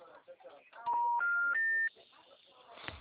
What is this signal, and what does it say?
Special information tone, with speech.